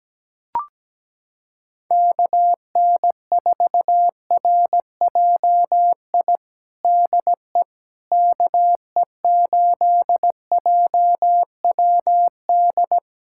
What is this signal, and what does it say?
Morse code, heard as a keyed tone, about 17 wpm, 700 Hz. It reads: KN4RJI DE KE8JWD
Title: USA Callsign DE Callsign - 6 Characters - 17wpm